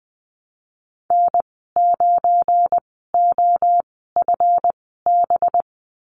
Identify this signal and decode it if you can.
Morse code, heard as a keyed tone, about 20 wpm, spaced as for 15 wpm, 700 Hz. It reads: N9OFB
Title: 193 Lesson 31g - Letter v - Call Signs 20wpm